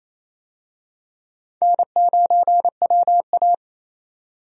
Morse code, heard as a keyed tone, about 28 wpm, 700 Hz. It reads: N9WA
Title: USA Call Signs - 4 Characters - 28wpm